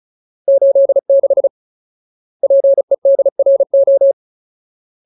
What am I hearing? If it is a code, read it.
Morse code, heard as a keyed tone, about 34 wpm, 550 Hz. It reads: 86 PEDRO
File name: Ultimate Word List / 35WPM Ultimate